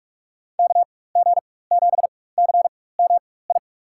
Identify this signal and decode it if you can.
Morse code, heard as a keyed tone, about 46 wpm, spaced as for 21 wpm, 700 Hz. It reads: KG7CMI
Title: USA Call Signs - 6 Characters - 4x Character Spacing - 45wpm